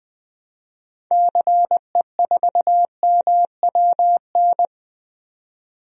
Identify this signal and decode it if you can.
Morse code, heard as a keyed tone, about 20 wpm, 700 Hz. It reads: CE4MWN
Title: DXCC-100 Call Signs - 6 Characters - 20wpm